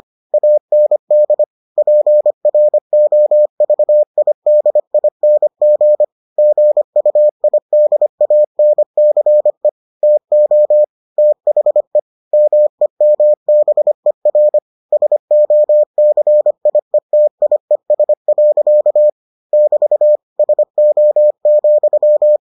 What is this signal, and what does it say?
Morse code, heard as a keyed tone, about 25 wpm, 600 Hz. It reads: AND PROVIDING GUIDANCE TO THE MEMBER SOCIETIES. <BT> SO,